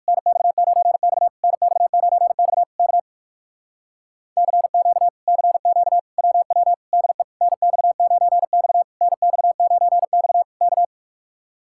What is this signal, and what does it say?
Morse code, heard as a keyed tone, about 64 wpm, spaced as for 55 wpm, 700 Hz. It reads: NX9X NX9X K CQ CQ WW DE NX9X NX9X K
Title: Speed-Racing Common QSO Phrases Encore - 45wpm